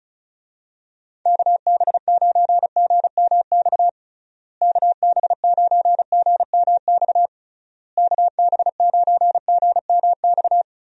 Morse code, heard as a keyed tone, about 36 wpm, 700 Hz. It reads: KB9GMX KB9GMX KB9GMX
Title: USA Call Signs - 6 Characters - Repeated 3x - 35wpm